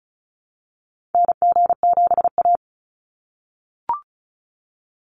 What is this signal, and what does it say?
Morse code, heard as a keyed tone, about 36 wpm, 700 Hz. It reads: NG7A
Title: NA SOTA Activator Callsigns 35wpm